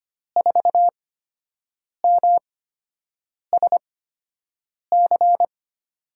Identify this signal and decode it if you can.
Morse code, heard as a keyed tone, about 25 wpm, spaced as for 7 wpm, 700 Hz. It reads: 4MSC
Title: USA Call Signs - 8x Character Spacing - 25wpm